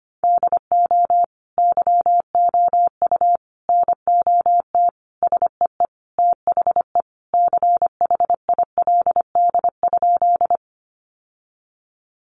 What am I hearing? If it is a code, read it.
Morse code, heard as a keyed tone, about 25 wpm, 700 Hz. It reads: DO YOU NOT SEE THE CHILD?
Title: Sentences from Top 800 Words 25wpm